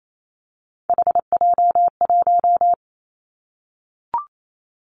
Morse code, heard as a keyed tone, about 28 wpm, 700 Hz. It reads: HJ1
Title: Three Challenging Letter-Number Combinations 28wpm